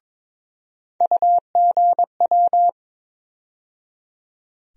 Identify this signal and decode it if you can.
Morse code, heard as a keyed tone, about 22 wpm, 700 Hz. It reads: UGW
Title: USA Call Sign Postfixes - 3 Letters 22wpm